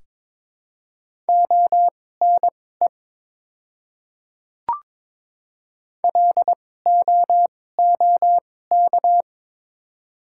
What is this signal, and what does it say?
Morse code, heard as a keyed tone, about 22 wpm, spaced as for 16 wpm, 700 Hz. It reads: ONE LOOK
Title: 2x Character Spacing - Top 300 Words - 22wpm